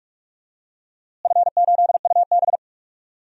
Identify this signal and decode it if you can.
Morse code, heard as a keyed tone, about 46 wpm, 700 Hz. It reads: F8UB
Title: DXCC-100 Call Signs - 4 Characters - Repeated 2x - 45wpm